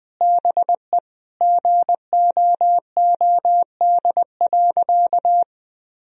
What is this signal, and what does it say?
Morse code, heard as a keyed tone, about 20 wpm, 700 Hz. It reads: BE GOOD.
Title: Sentences from Top 1000 Words - Review Part 1 - 20wpm